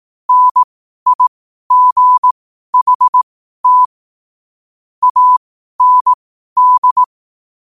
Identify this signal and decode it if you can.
Morse code, heard as a keyed tone, about 18 wpm, spaced as for 12 wpm, 1000 Hz. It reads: NIGHT AND